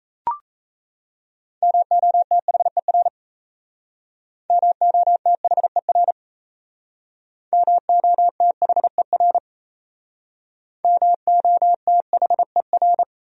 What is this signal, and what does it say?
Morse code, heard as a keyed tone, about 36 wpm, 700 Hz. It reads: MOTHER MOTHER MOTHER MOTHER